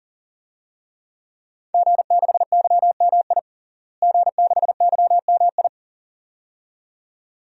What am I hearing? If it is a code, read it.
Morse code, heard as a keyed tone, about 40 wpm, 700 Hz. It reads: G6YMI G6YMI
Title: DXCC-100 Call Signs - 5 Characters - Repeated 2x - 40wpm